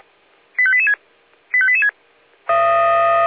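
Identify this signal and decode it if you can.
Five-tone selective call (ZVEI), heard as quick runs of five tones, tones of about 70 ms. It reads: 85886 85886
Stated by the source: ZVEI five-tone selective call received on an SDR.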